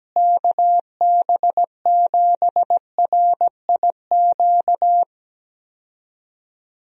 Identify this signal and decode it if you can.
Morse code, heard as a keyed tone, about 17 wpm, 700 Hz. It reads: KB7RIQ
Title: USA Call Signs - 6 Characters - Repeated 3x - 17wpm